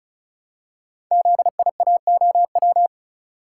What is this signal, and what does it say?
Morse code, heard as a keyed tone, about 36 wpm, 700 Hz. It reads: ZIAOW
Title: Code Groups - 5 Letter:Numbers - 35wpm